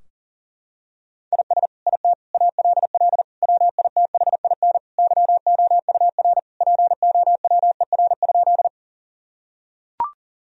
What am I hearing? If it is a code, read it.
Morse code, heard as a keyed tone, about 40 wpm, 700 Hz. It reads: IS IT ALL WITHIN YOUR POWER?
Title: Sentences from Top 1000 Words - Review Part 2 - 40wpm